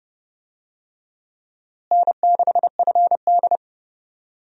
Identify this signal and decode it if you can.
Morse code, heard as a keyed tone, about 30 wpm, 700 Hz. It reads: N6FD
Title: USA Call Signs - 4 Characters - Repeated 2x - 30wpm